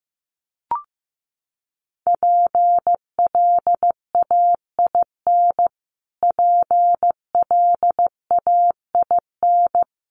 Morse code, heard as a keyed tone, about 15 wpm, 700 Hz. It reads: PLAIN PLAIN
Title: Top 500 Words Repeated 15wpm